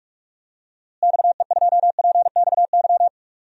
Morse code, heard as a keyed tone, about 45 wpm, 700 Hz. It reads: XE2PXY
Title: DXCC-100 Call Signs - 6 Characters - Repeated 3x - 45wpm